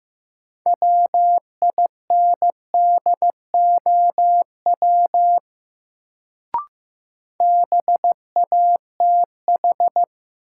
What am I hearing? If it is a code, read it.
Morse code, heard as a keyed tone, about 15 wpm, 700 Hz. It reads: WINDOW BATH
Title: Sentences from Top 1600 Words - New Word Drill - 15wpm